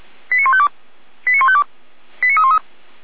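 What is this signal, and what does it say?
Five-tone selective call (ZVEI), heard as quick runs of five tones, tones of about 70 ms. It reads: 89152 89152 89313
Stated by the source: ZVEI five-tone selective call 89337, received by SDR, from the feuerwehr-orlamuende fire brigade audio server.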